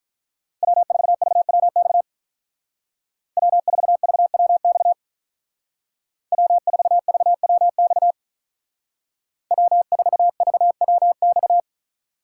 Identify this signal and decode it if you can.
Morse code, heard as a keyed tone, about 41 wpm, 700 Hz. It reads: W4VWX W4VWX W4VWX W4VWX
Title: Speed Racing - USA 1x3 Call Signs 35wpm